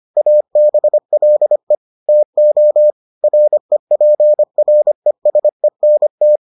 Morse code, heard as a keyed tone, about 25 wpm, 600 Hz. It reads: ABLE TO REPRESENT